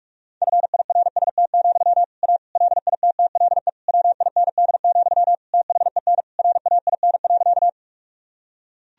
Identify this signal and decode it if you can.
Morse code, heard as a keyed tone, about 41 wpm, 700 Hz. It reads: FIRST, A LITTLE WIND, THEN RAIN.
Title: Sentences from Top 1000 Words - Review Part 2 - 45wpm